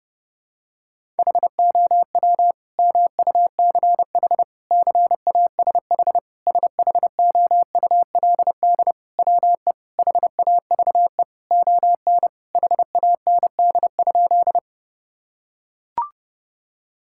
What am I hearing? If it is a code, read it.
Morse code, heard as a keyed tone, about 30 wpm, 700 Hz. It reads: HOW MUCH CASH SHOULD WE HAVE ON HAND?
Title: Sentences from Top 1000 Words - Review Part 5 - 30wpm